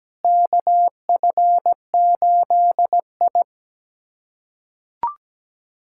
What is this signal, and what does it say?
Morse code, heard as a keyed tone, about 17 wpm, 700 Hz. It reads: KF8I
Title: USA Callsign DE Callsign - 4 Characters - 17wpm